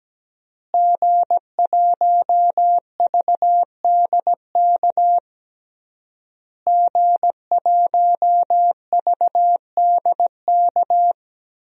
Morse code, heard as a keyed tone, about 17 wpm, 700 Hz. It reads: G1VDK G1VDK
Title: DXCC-100 Call Signs - 5 Characters - Repeated 3x - 17wpm